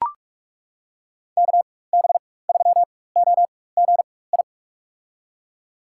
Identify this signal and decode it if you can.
Morse code, heard as a keyed tone, about 42 wpm, spaced as for 20 wpm, 700 Hz. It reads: KB3OGI
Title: USA Call Signs - 6 Characters - 4x Character Spacing - 45wpm